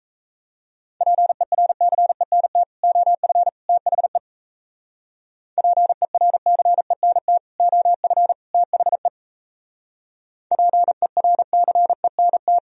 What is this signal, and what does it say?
Morse code, heard as a keyed tone, about 36 wpm, 700 Hz. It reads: PERCENT OF THE PERCENT OF THE PERCENT